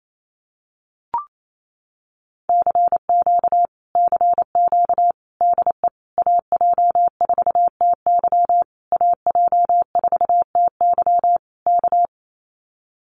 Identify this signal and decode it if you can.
Morse code, heard as a keyed tone, about 28 wpm, 700 Hz. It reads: CQ CQ DE AJ4TY AJ4TY K
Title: Common QSO Phrases Encore 28wpm